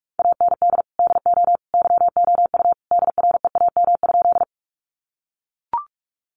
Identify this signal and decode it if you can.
Morse code, heard as a keyed tone, about 44 wpm, 700 Hz. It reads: AND DO YOU DREAM?
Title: Sentences from Top 1500 Words 45wpm